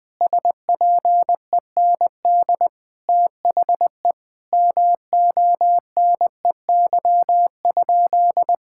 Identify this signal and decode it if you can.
Morse code, heard as a keyed tone, about 20 wpm, 700 Hz. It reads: SPEND THE MONEY?
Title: Sentences from Top 1000 Words - Review Part 5 - 20wpm